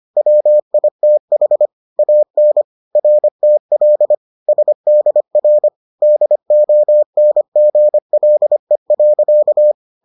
Morse code, heard as a keyed tone, about 25 wpm, 600 Hz. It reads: WITH AN RTL SDR DONGLE.